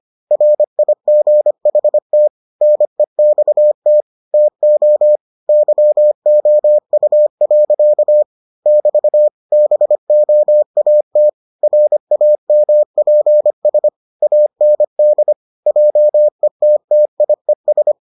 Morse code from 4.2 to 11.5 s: TO YOU. <BT> BOAT

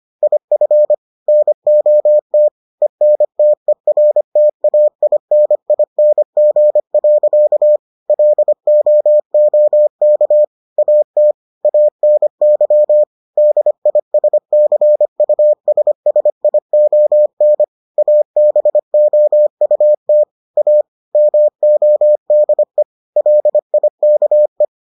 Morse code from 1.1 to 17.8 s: NOT ENTERTAINING. LOOK AT ANY DISCUSSION

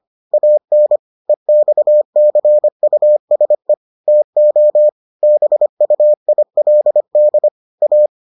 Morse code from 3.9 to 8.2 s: TO BUILD A

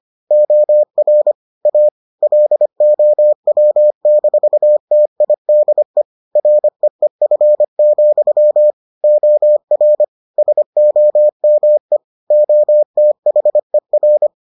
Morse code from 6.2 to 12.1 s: REEF, OR SOME

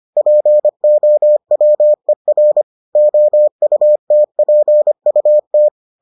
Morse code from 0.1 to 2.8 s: POWER

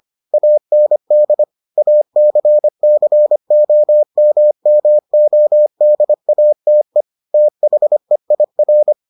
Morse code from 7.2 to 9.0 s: THEIR

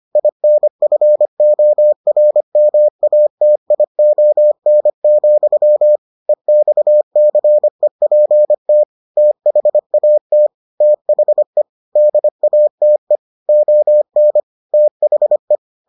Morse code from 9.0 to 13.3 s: THAT THE DATE

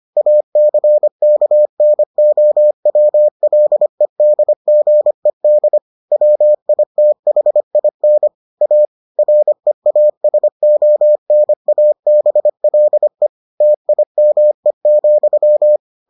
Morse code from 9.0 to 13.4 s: REASONABLE

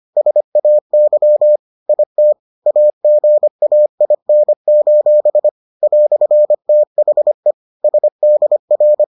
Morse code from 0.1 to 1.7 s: SAY

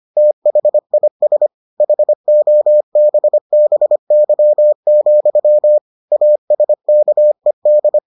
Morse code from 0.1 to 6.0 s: THIS HOBBY,

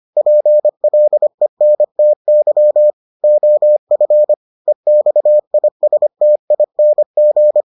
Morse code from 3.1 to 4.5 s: OF

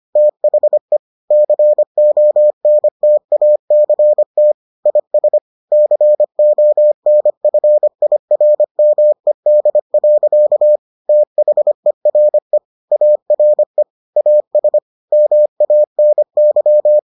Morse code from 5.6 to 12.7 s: CONFIRMED. THERE

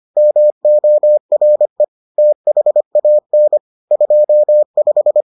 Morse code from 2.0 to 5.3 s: THAN 25

